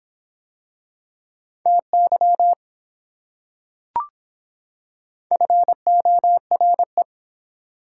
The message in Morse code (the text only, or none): TY FORE